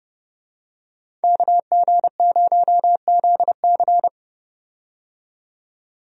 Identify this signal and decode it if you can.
Morse code, heard as a keyed tone, about 30 wpm, 700 Hz. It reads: KG0ZC
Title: USA Call Signs - 5 Characters - 30wpm